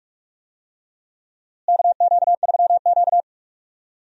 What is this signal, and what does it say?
Morse code, heard as a keyed tone, about 43 wpm, 700 Hz. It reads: KQ3Q
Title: USA Call Signs - 4 Characters - Repeated 3x - 45wpm